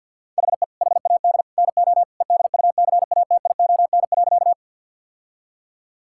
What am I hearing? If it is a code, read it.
Morse code, heard as a keyed tone, about 50 wpm, 700 Hz. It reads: HE HAD NO EDUCATION.
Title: Sentences from Top 1000 Words - Review Part 6 - 50wpm